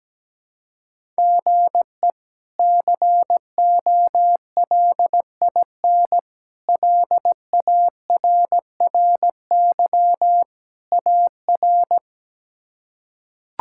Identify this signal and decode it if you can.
Morse code, heard as a keyed tone, about 17 wpm, 700 Hz. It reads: GE COLIN LARRY AR
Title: K1USN SST QSO Practice - 17wpm